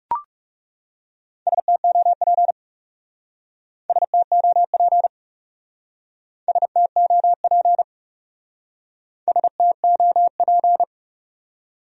Morse code, heard as a keyed tone, about 39 wpm, 700 Hz. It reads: STOP STOP STOP STOP